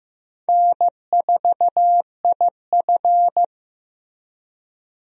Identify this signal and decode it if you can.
Morse code, heard as a keyed tone, about 15 wpm, 700 Hz. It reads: N4IF
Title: USA Call Signs - 4 Characters - Repeated 2x - 15wpm